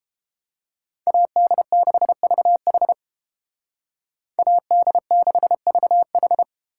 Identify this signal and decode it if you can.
Morse code, heard as a keyed tone, about 31 wpm, 700 Hz. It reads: AD6VH AD6VH